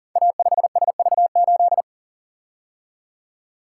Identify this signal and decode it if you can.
Morse code, heard as a keyed tone, about 39 wpm, 700 Hz. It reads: A5SV8
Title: Code Groups - 5 Letter:Numbers - 40wpm